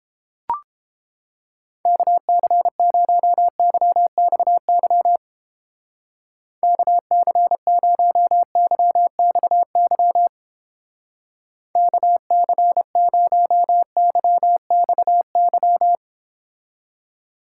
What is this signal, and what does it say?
Morse code, heard as a keyed tone, about 29 wpm, 700 Hz. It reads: KC0YXY KC0YXY KC0YXY